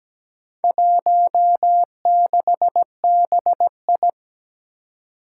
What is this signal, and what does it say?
Morse code, heard as a keyed tone, about 17 wpm, 700 Hz. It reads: 16BI
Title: Four Challenging Letter-Number Combinations 17wpm